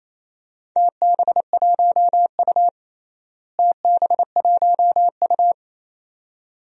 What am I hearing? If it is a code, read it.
Morse code, heard as a keyed tone, about 28 wpm, 700 Hz. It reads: TB1U TB1U
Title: DXCC-100 Call Signs - 4 Characters - Repeated 2x - 28wpm